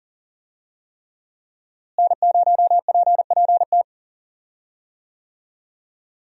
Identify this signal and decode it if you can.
Morse code, heard as a keyed tone, about 39 wpm, 700 Hz. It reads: N0PPT
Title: USA Call Signs - 5 Characters - Repeated 3x - 40wpm